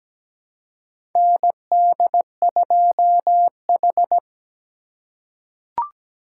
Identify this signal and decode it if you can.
Morse code, heard as a keyed tone, about 17 wpm, 700 Hz. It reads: ND2H
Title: USA Call Signs - 3x Character Spacing - 17wpm